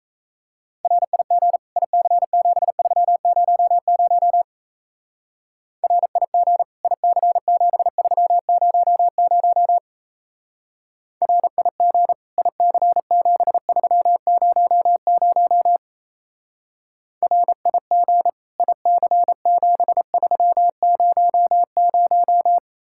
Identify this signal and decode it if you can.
Morse code, heard as a keyed tone, about 36 wpm, 700 Hz. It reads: RIG IC7300 RIG IC7300 RIG IC7300 RIG IC7300